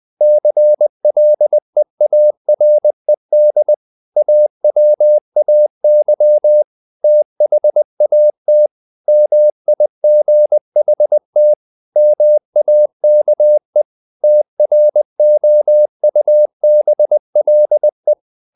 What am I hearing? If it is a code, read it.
Morse code, heard as a keyed tone, about 20 wpm, 600 Hz. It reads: CLEARED AWAY THAT MIGHT MAKE TROUBLE
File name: the old man and the sea / the old man and the sea part 1 20wpm